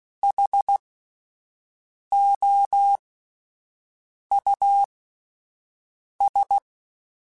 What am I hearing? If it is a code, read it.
Morse code, heard as a keyed tone, about 16 wpm, spaced as for 5 wpm, 800 Hz. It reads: HOUS
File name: General / General-mp3s / Houses @10